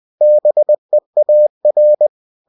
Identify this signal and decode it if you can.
Morse code, heard as a keyed tone, about 20 wpm, 600 Hz. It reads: BEAR
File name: cw-books / winnie-the-pooh / winnie-20wpm-page-0006